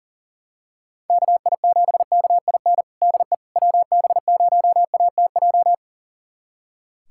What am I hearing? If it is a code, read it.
Morse code, heard as a keyed tone, about 40 wpm, 700 Hz. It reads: KI7KIN DE WB0ATJ
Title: USA Callsign DE Callsign 40wpm